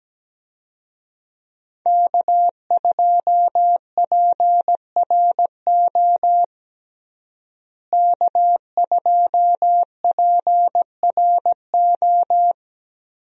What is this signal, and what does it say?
Morse code, heard as a keyed tone, about 17 wpm, 700 Hz. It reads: K2PRO K2PRO
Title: USA Call Signs - 5 Characters - Repeated 3x - 17wpm